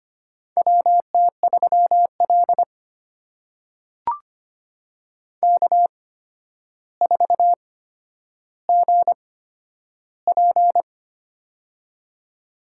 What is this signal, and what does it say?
Morse code, heard as a keyed tone, about 25 wpm, spaced as for 11 wpm, 700 Hz. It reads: WT3L K4GP
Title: USA Call Signs - 8x Character Spacing - 25wpm